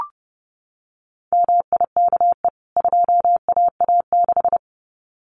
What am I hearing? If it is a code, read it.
Morse code, heard as a keyed tone, about 30 wpm, 700 Hz. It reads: MIKE 2AA6
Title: CWT Exchange 30wpm